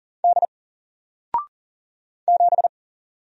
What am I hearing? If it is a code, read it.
Morse code, heard as a keyed tone, about 40 wpm, 700 Hz. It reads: D 7
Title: Single Letter-Number Alt 40wpm